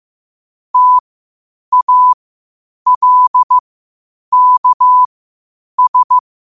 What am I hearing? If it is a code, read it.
Morse code, heard as a keyed tone, about 15 wpm, spaced as for 8 wpm, 1000 Hz. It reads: TALKS
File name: qotdfiles / 8wpm